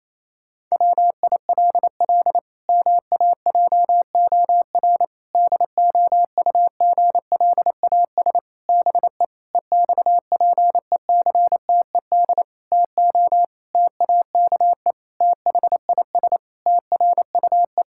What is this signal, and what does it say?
Morse code, heard as a keyed tone, about 28 wpm, 700 Hz. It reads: WILL MAJOR DOUGLAS BE EXPECTED TO TAKE THIS TRUE
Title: Pangram Challenge v2 - 28wpm